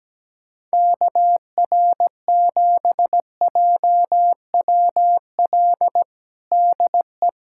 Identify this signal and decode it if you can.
Morse code, heard as a keyed tone, about 17 wpm, 700 Hz. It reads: KR7JWL DE
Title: USA Callsign DE Callsign - 6 Characters - 17wpm